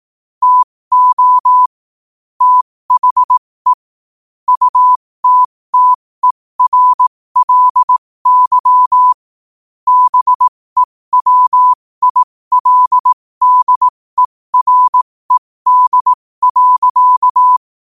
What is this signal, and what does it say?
Morse code, heard as a keyed tone, about 18 wpm, spaced as for 15 wpm, 1000 Hz. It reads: TO THE UTTERLY BEWILDERED.
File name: qotdfiles / 15wpm